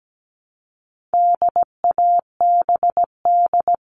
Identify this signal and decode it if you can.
Morse code, heard as a keyed tone, about 17 wpm, 700 Hz. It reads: DABD
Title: All Cut Numbers - 4-Digit 17wpm